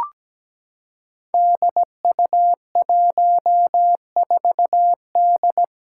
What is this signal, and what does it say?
Morse code, heard as a keyed tone, about 17 wpm, 700 Hz. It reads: DU14D
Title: Five Challenging Letter-Number Combinations 17wpm